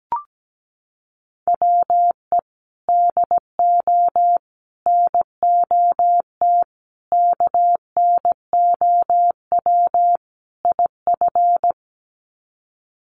Morse code, heard as a keyed tone, about 17 wpm, 700 Hz. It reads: WE DO NOT KNOW IF
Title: Sets of 5 Words Encore2 17wpm